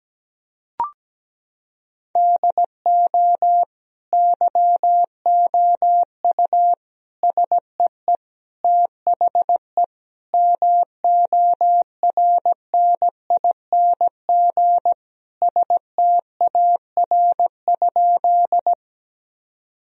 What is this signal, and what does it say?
Morse code, heard as a keyed tone, about 17 wpm, 700 Hz. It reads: DO YOU SEE THE MORNING STAR?
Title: Sentences from Top 1000 Words - Review Part 2 - 17wpm